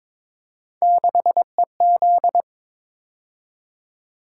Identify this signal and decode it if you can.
Morse code, heard as a keyed tone, about 23 wpm, 700 Hz. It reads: 6EZ
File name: practice / 22wpm / Three Letter-Number Combinations - 22wpm